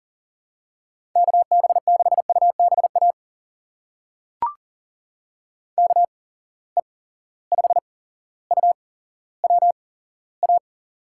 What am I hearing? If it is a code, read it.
Morse code, heard as a keyed tone, about 40 wpm, spaced as for 17 wpm, 700 Hz. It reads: KB6UBA KE5UWA
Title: USA Call Signs - 6 Characters - 8x Character Spacing - 40wpm